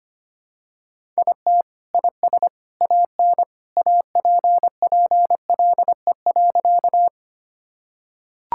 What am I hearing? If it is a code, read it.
Morse code, heard as a keyed tone, about 25 wpm, 700 Hz. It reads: IT IS AN APPLE.
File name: Sentences with 2-Word Phrases - 25wpm